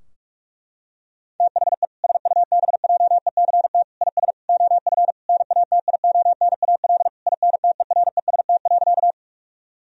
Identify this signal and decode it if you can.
Morse code, heard as a keyed tone, about 47 wpm, 700 Hz. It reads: THE SUBJECT IS OF NATIONAL INTEREST.
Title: Sentences from Top 1000 Words - Review Part 4 - 45wpm